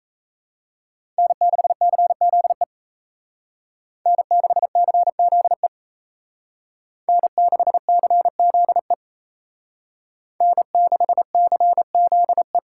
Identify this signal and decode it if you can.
Morse code, heard as a keyed tone, about 33 wpm, 700 Hz. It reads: N6CZE N6CZE N6CZE N6CZE
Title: Speed Racing - USA 1x3 Call Signs 28wpm